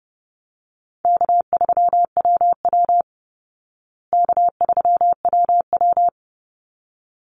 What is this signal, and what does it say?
Morse code, heard as a keyed tone, about 30 wpm, 700 Hz. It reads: K3WW K3WW
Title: Easy Call Signs Repeated 2x - 30wpm